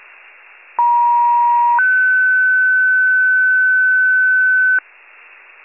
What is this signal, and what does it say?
Two-tone page, heard as a one-second tone, then a three-second tone, about 950 then 1500 Hz, with noise. It is heard from 0.8 to 4.8 s.
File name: wcalert